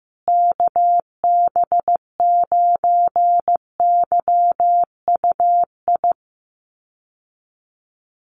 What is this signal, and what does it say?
Morse code, heard as a keyed tone, about 15 wpm, 700 Hz. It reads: KB9YUI
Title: USA Call Signs Repeated 3x 15wpm